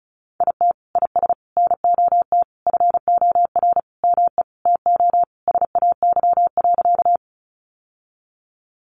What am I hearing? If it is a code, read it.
Morse code, heard as a keyed tone, about 34 wpm, 700 Hz. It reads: IT IS NOT FOR ME TO SAY.
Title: Sentences from Top 600 Words 35wpm